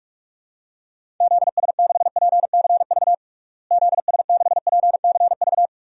Morse code, heard as a keyed tone, about 47 wpm, 700 Hz. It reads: ZS6PCV ZS6PCV
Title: DXCC-100 Call Signs - 6 Characters - Repeated 2x - 45wpm